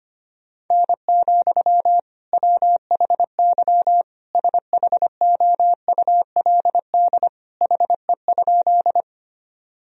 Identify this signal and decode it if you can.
Morse code, heard as a keyed tone, about 25 wpm, 700 Hz. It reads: N, WHY SHOULD HE?
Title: Sentences from Top 300 Words and Abbreviations 25wpm